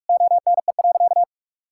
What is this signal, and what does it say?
Morse code, heard as a keyed tone, about 46 wpm, 700 Hz. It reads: ONE.
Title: Sentences from Top 1000 Words - Review Part 5 - 45wpm